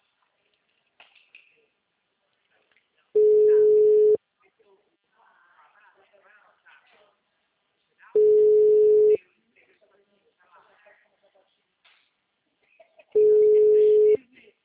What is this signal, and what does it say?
Ringback tone (Europe, 425 Hz), with speech.
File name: out- -2013- - - .77339